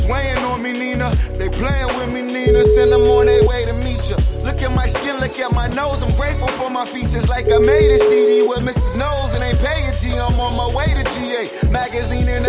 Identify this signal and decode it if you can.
Ringback tone (Europe, 425 Hz), with noise.